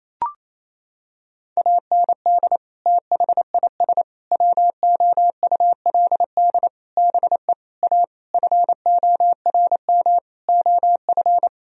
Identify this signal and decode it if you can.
Morse code, heard as a keyed tone, about 28 wpm, 700 Hz. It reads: AND THIS WOULD BE A FORM OF
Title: Sentences from Top 1000 Words - Review Part 1 - 28wpm